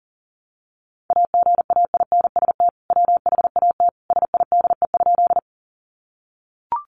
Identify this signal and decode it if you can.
Morse code, heard as a keyed tone, about 40 wpm, 700 Hz. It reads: AGAINST WHAT SIDE?
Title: Sentences from Top 400 Words 40wpm